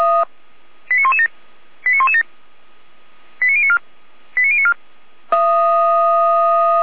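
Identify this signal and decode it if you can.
Five-tone selective call (ZVEI), heard as quick runs of five tones, tones of about 70 ms. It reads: 89117 89117 89094 89094
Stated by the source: ZVEI five-tone selective call received on an SDR.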